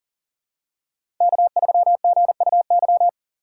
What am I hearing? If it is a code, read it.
Morse code, heard as a keyed tone, about 40 wpm, 700 Hz. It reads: K3GUY
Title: USA Call Signs - 5 Characters - 3x Character Spacing - 40wpm